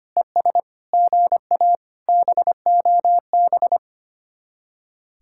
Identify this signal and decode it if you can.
Morse code, heard as a keyed tone, about 30 wpm, 700 Hz. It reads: ES GA BOB
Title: Speed-Racing Common QSO Phrases - 25wpm